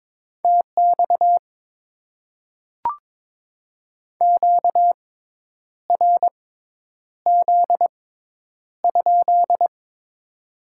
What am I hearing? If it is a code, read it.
Morse code, heard as a keyed tone, about 22 wpm, spaced as for 9 wpm, 700 Hz. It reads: TX QRZ?
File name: Top 90 QSO Elements - 6x Character Spacing - 22wpm